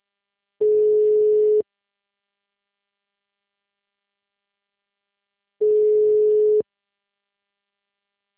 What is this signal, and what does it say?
Ringback tone (Europe, 425 Hz).